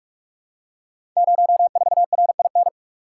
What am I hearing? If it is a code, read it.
Morse code, heard as a keyed tone, about 44 wpm, 700 Hz. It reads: 04RIN
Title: Code Groups - 5 Letter:Numbers - 45wpm